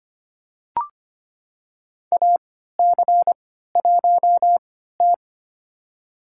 Morse code, heard as a keyed tone, about 25 wpm, spaced as for 14 wpm, 700 Hz. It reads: AC1T
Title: USA Call Signs - 4 Characters - 3x Character Spacing - 25wpm